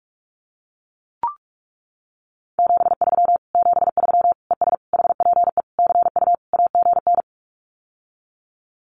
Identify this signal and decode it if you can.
Morse code, heard as a keyed tone, about 44 wpm, 700 Hz. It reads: 73 73 ES HPE CU AGN
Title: Common QSO Phrases Encore 45wpm